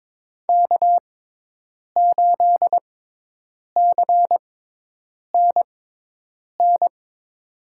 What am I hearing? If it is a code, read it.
Morse code, heard as a keyed tone, about 22 wpm, spaced as for 8 wpm, 700 Hz. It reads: K8CNN
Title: USA Call Signs - 6x Character Spacing - 22wpm